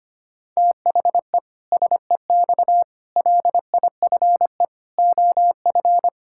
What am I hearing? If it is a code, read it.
Morse code, heard as a keyed tone, about 25 wpm, 700 Hz. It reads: THE SEX LIFE OF
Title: Pangram Challenge v2 - 25wpm